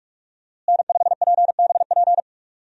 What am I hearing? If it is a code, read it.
Morse code, heard as a keyed tone, about 41 wpm, 700 Hz. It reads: N5PBP
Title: USA Call Signs - 5 Characters - Repeated 3x - 45wpm